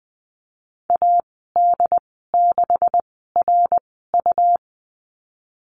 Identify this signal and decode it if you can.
Morse code, heard as a keyed tone, about 20 wpm, spaced as for 15 wpm, 700 Hz. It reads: AD6RU
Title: 154 Lesson 25g - Letter p - Call Signs 20wpm